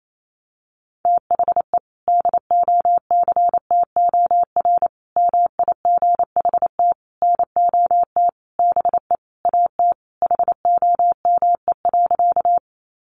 Morse code, heard as a keyed tone, about 28 wpm, 700 Hz. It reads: THE DOCTOR MIGHT NOT BE AT HOME.
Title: Sentences from Top 1500 Words 28wpm